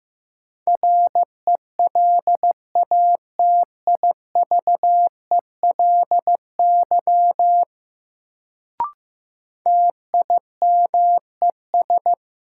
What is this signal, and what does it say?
Morse code, heard as a keyed tone, about 15 wpm, 700 Hz. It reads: RELATIVELY TIMES
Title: Sentences from Top 1100 Words - New Word Drill Alt - 15wpm